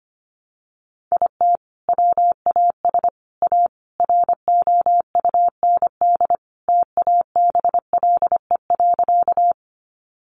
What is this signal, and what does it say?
Morse code, heard as a keyed tone, about 25 wpm, 700 Hz. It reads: IT WAS A ROUND TABLE.
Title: Sentences from Top 400 Words 25wpm